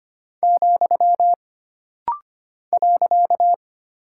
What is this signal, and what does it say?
Morse code, heard as a keyed tone, about 25 wpm, 700 Hz. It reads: , .
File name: Single Punctuation-Prosign - Rapid-Fire - 25wpm